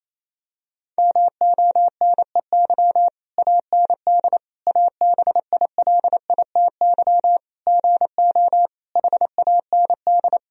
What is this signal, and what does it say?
Morse code, heard as a keyed tone, about 27 wpm, 700 Hz. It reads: MONEY AND ABILITY GO HAND
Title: Sentences from Top 1000 Words - Review Part 3 - 28wpm